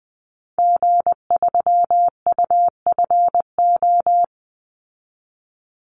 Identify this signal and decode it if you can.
Morse code, heard as a keyed tone, about 20 wpm, 700 Hz. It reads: G3UFO
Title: Call Signs Repeated 3x - 20wpm